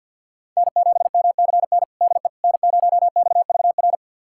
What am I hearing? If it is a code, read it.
Morse code, heard as a keyed tone, about 46 wpm, 700 Hz. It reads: N7MCN DE N0XVR
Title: USA Callsign DE Callsign - 5 Characters - 50wpm